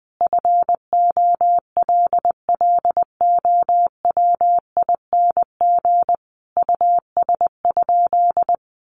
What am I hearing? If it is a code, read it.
Morse code, heard as a keyed tone, about 20 wpm, 700 Hz. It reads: FOLLOWING US?
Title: Sentences from Top 700 Words 20wpm